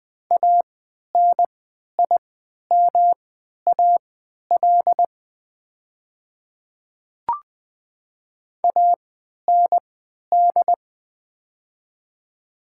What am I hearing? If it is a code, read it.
Morse code, heard as a keyed tone, about 20 wpm, spaced as for 11 wpm, 700 Hz. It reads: ANIMAL AND
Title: 3x Character Spacing - Top 200 Words - 20wpm